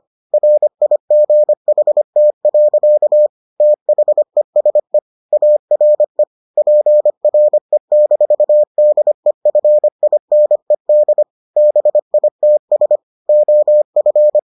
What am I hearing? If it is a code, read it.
Morse code, heard as a keyed tone, about 25 wpm, 600 Hz. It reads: RIGHT. THESE ARE PRE-DEFINED BITS OF